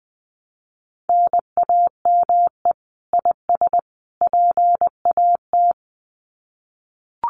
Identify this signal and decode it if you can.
Morse code, heard as a keyed tone, about 20 wpm, 700 Hz. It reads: NAME IS PAT
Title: Common QSO Phrases 20wpm